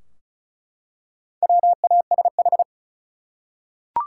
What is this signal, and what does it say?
Morse code, heard as a keyed tone, about 34 wpm, 700 Hz. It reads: WASH
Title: Four Letter Words - 35wpm